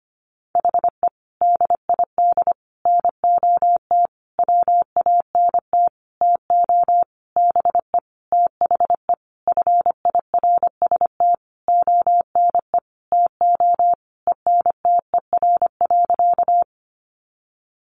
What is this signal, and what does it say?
Morse code, heard as a keyed tone, about 25 wpm, 700 Hz. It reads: HE DID NOT WANT TO BE THE FIRST ONE TO ENTER.
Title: Sentences from Top 1300 Words 25wpm